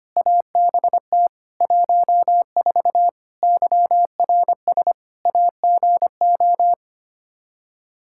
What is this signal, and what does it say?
Morse code, heard as a keyed tone, about 25 wpm, 700 Hz. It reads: ABT 14 YRS AGO
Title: Practice QSO - Exchange 3 - More Info - 25wpm